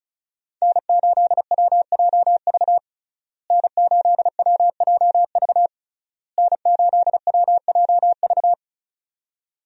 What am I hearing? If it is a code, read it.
Morse code, heard as a keyed tone, about 36 wpm, 700 Hz. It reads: N8WJV N8WJV N8WJV
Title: USA Call Signs - 5 Characters - Repeated 3x - 35wpm